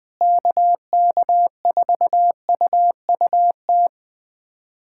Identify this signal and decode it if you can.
Morse code, heard as a keyed tone, about 20 wpm, 700 Hz. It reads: KK4UUT
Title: USA Callsign DE Callsign - 6 Characters - 20wpm